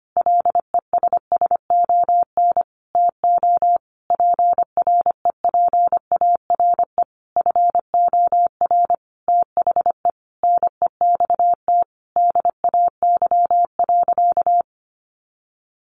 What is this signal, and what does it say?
Morse code, heard as a keyed tone, about 25 wpm, 700 Hz. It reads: LESSON TO PREPARE FOR THE NEXT DAY.
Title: Sentences from Top 1200 Words 25wpm